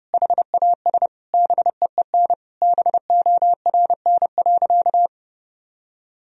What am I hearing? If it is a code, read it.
Morse code, heard as a keyed tone, about 31 wpm, 700 Hz. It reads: HAS BEEN BORN.
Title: Sentences from Top 1000 Words - Review Part 2 - 30wpm